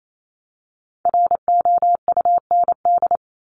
Morse code, heard as a keyed tone, about 28 wpm, 700 Hz. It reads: ROUND
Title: Top 200 Words Repeated 28wpm